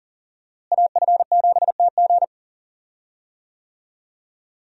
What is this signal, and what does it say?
Morse code, heard as a keyed tone, about 40 wpm, 700 Hz. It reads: AF7TG
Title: USA Call Signs - 5 Characters - Repeated 3x - 40wpm